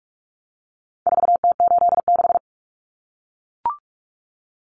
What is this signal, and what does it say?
Morse code, heard as a keyed tone, about 44 wpm, 700 Hz. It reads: 4T86
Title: Common Cut Numbers - 4-Digit 45wpm